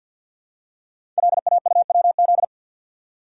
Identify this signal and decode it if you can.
Morse code, heard as a keyed tone, about 52 wpm, 700 Hz. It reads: LAUWZ
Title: Code Groups - 5 Letter:Numbers - 50wpm